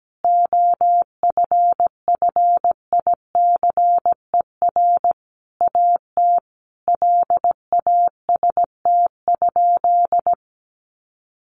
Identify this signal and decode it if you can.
Morse code, heard as a keyed tone, about 17 wpm, 700 Hz. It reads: OFFICER AT LAST?
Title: Sentences from Top 1300 Words 17wpm